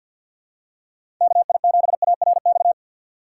Morse code, heard as a keyed tone, about 47 wpm, 700 Hz. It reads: KI7ARX
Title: USA Call Signs - 6 Characters - 6x Character Spacing - 50wpm